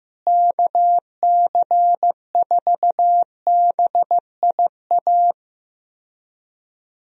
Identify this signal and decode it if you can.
Morse code, heard as a keyed tone, about 15 wpm, 700 Hz. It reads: KC4BIA
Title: USA Call Signs - 6 Characters - Repeated 2x - 15wpm